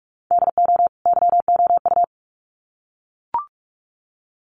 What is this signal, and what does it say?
Morse code, heard as a keyed tone, about 44 wpm, 700 Hz. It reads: DO YOU
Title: Sets of 2 Words 45wpm